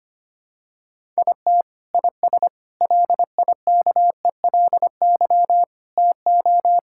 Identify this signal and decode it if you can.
Morse code, heard as a keyed tone, about 25 wpm, 700 Hz. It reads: IT IS LIKELY TO
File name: practice / 25wpm / Sentences with 2-Word Phrases - 25wpm